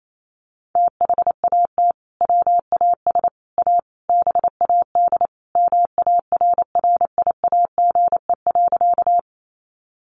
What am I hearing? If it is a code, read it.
Morse code, heard as a keyed tone, about 28 wpm, 700 Hz. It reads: THAT WAS A BAD MARRIAGE.
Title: Sentences from Top 800 Words 28wpm